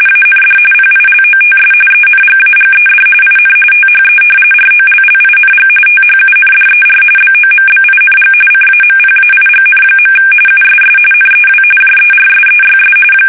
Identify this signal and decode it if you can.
RTTY, about 45.45 baud. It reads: AND PILOTS ENGAGED IN THE ROLLING THUNDER BOMBING CAMPAIGN. USS NEW JERSEYS
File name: fsk850 / battleshipnj